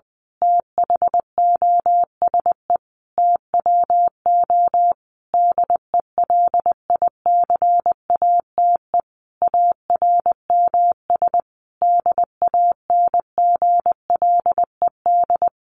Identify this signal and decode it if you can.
Morse code, heard as a keyed tone, about 20 wpm, 700 Hz. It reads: THOSE TWO DELICATE ARMS DANGLED
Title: A Sound Of Thunder 20wpm